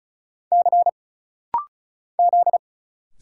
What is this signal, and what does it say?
Morse code, heard as a keyed tone, about 36 wpm, 700 Hz. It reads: C Z
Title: Single Letters - Rapid-Fire 35wpm